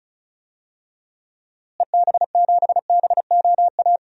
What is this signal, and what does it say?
Morse code, heard as a keyed tone, about 34 wpm, 700 Hz. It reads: EB7BOA
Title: DXCC-100 Call Signs - 6 Characters - Repeated 3x - 35wpm